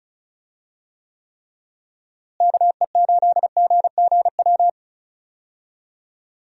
Morse code, heard as a keyed tone, about 36 wpm, 700 Hz. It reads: KE8GGW
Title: USA Call Signs - 6 Characters - Repeated 3x - 35wpm